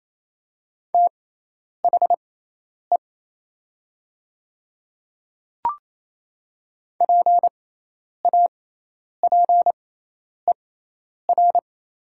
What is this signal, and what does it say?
Morse code, heard as a keyed tone, about 28 wpm, spaced as for 10 wpm, 700 Hz. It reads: THE PAPER
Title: 6x Character Spacing - Top 300 Words - 28wpm